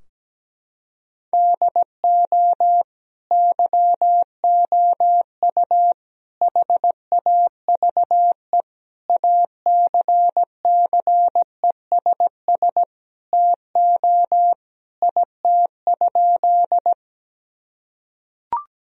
Morse code, heard as a keyed tone, about 17 wpm, 700 Hz. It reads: DO YOU HAVE ACCESS TO IT?
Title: Sentences from Top 1000 Words - Review Part 4 - 17wpm